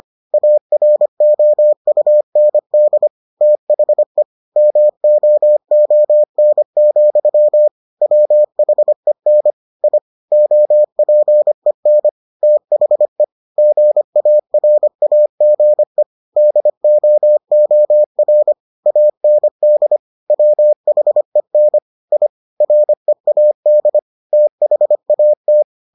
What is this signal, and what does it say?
Morse code, heard as a keyed tone, about 25 wpm, 600 Hz. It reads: AROUND THE MOON, WHEN I OPEN THE GARAGE DOOR AND WHEN I READ THAT